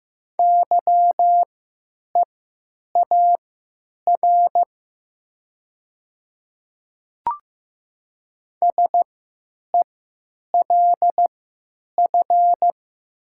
Morse code, heard as a keyed tone, about 15 wpm, spaced as for 9 wpm, 690 Hz. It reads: YEAR SELF
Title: 3x Character Spacing - Top 300 Words - 15wpm